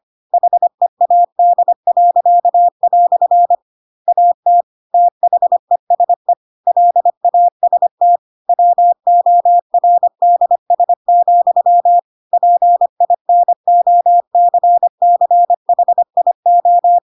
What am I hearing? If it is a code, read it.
Morse code, heard as a keyed tone, about 25 wpm, 700 Hz. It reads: HEAD." AT THESE LAST WORDS, PINOCCHIO
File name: Pinocchio / Pinocchio ch4 25 25 ik1hge